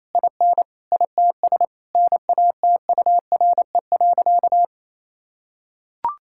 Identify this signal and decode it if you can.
Morse code, heard as a keyed tone, about 28 wpm, 700 Hz. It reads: IN ITS NATURE.
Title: Sentences from Top 1000 Words - Review Part 6 - 28wpm